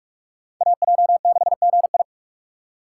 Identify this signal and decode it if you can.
Morse code, heard as a keyed tone, about 42 wpm, 690 Hz. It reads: AJ6GI
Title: USA Call Signs - 5 Characters - 4x Character Spacing - 45wpm